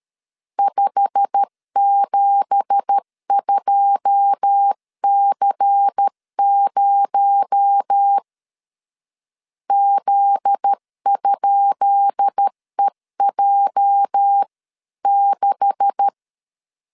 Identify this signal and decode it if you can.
Morse code, heard as a keyed tone, about 13 wpm, 800 Hz. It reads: 572C0 Z?EJ6
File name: LEZIONI / 10 SIGNS / 10B INT 12wpm 2414